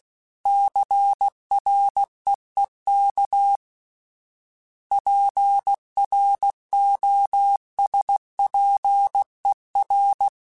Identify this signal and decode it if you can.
Morse code, heard as a keyed tone, about 16 wpm, 800 Hz. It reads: CREEK PROSPER
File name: fromSMs / FromSMs mp3s / Cities@15